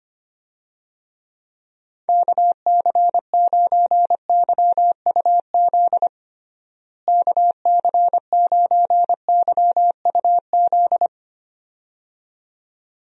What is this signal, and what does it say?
Morse code, heard as a keyed tone, about 25 wpm, 700 Hz. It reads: KC9YUZ KC9YUZ
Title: USA Call Signs - 6 Characters - Repeated 2x - 25wpm